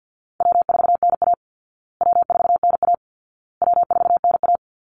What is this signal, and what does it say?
Morse code, heard as a keyed tone, about 50 wpm, 700 Hz. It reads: W4NA W4NA W4NA
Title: Call Signs Repeated 2x - 50wpm